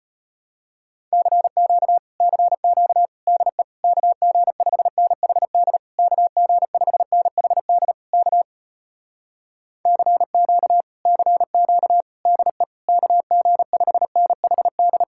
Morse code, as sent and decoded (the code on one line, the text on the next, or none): -.-. --.- / -.-. --.- / -.. / -.- --. ..... -. .... -.. / -.- --. ..... -. .... -.. / -.- / -.-. --.- / -.-. --.- / -.. / -.- --. ..... -. .... -..
CQ CQ DE KG5NHD KG5NHD K CQ CQ DE KG5NHD